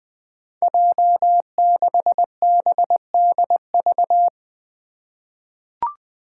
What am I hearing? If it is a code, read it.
Morse code, heard as a keyed tone, about 20 wpm, 700 Hz. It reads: J6BDV
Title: Five Challenging Letter-Number Combinations 20wpm